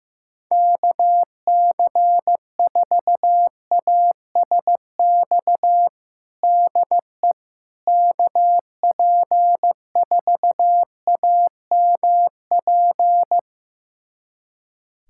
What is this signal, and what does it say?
Morse code, heard as a keyed tone, about 16 wpm, 700 Hz. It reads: KC4ASX DE KP4AMP
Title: USA Callsign DE Callsign 15wpm